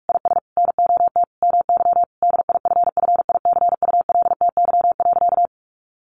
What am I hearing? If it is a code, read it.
Morse code, heard as a keyed tone, about 44 wpm, 700 Hz. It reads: IS NOT MY DIFFICULTY.
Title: Sentences from Top 1500 Words 45wpm